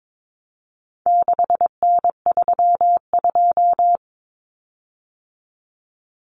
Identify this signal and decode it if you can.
Morse code, heard as a keyed tone, about 22 wpm, 700 Hz. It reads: 6N32
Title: Common Cut Numbers - 4-Digit 22wpm